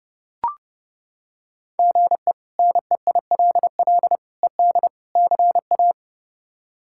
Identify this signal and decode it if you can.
Morse code, heard as a keyed tone, about 30 wpm, 700 Hz. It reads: GE NEILL ED CA
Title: K1USN SST QSO Practice - 30wpm